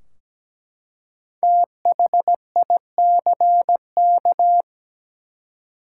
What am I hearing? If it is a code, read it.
Morse code, heard as a keyed tone, about 17 wpm, 700 Hz. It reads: THICK
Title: Five Letter Words - 17wpm